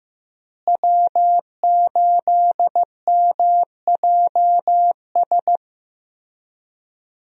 Morse code, heard as a keyed tone, about 16 wpm, 700 Hz. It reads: W8MJS
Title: USA Call Signs - 5 Characters - Repeated 2x - 15wpm